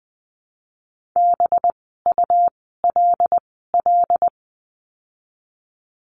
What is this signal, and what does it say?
Morse code, heard as a keyed tone, about 20 wpm, 700 Hz. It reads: B U L L
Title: 185 Lesson 30f - Letter b - Words 20wpm